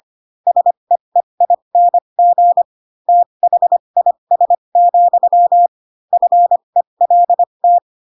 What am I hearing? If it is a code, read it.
Morse code, heard as a keyed tone, about 25 wpm, spaced as for 22 wpm, 700 Hz. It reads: SEEING THIS, FELT
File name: Pinocchio / Pinocchio ch3 18 25 ik1hge